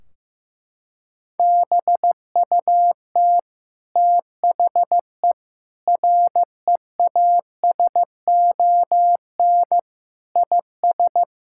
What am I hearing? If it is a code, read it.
Morse code, heard as a keyed tone, about 15 wpm, 700 Hz. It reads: BUT THE REASON IS
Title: Sentences from Top 1600 Words 15wpm